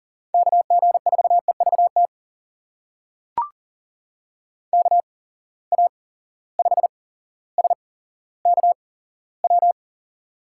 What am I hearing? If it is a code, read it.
Morse code, heard as a keyed tone, about 42 wpm, spaced as for 17 wpm, 700 Hz. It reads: KG4EVT KA5SKW
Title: USA Call Signs - 6 Characters - 8x Character Spacing - 40wpm